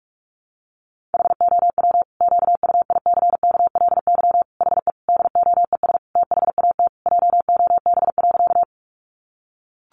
Morse code, heard as a keyed tone, about 44 wpm, 700 Hz. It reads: HOW QUICKLY HE DOES THAT JOB.
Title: Sentences from Top 700 Words 45wpm